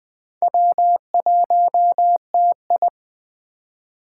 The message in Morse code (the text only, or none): W1TI